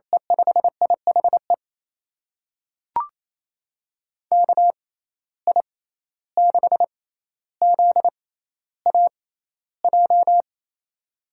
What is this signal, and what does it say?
Morse code, heard as a keyed tone, about 28 wpm, spaced as for 14 wpm, 700 Hz. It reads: E5IHE KI6ZAJ
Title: USA Call Signs - 6 Characters - 6x Character Spacing - 28wpm